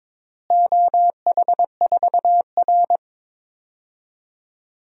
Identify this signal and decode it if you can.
Morse code, heard as a keyed tone, about 22 wpm, 700 Hz. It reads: OH4R
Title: DXCC-100 Call Signs - 4 Characters - Repeated 3x - 22wpm